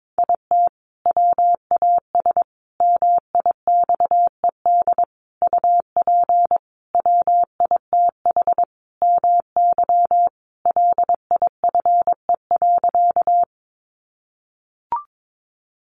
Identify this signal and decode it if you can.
Morse code, heard as a keyed tone, about 22 wpm, 700 Hz. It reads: IT WAS MIXED UP WITH MY LIFE.
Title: Sentences from Top 1500 Words 22wpm